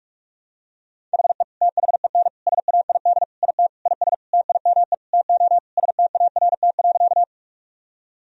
Morse code, heard as a keyed tone, about 41 wpm, 700 Hz. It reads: HE THEN SAID IT IS TIME TO START.
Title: Sentences from Top 1000 Words - Review Part 1 - 45wpm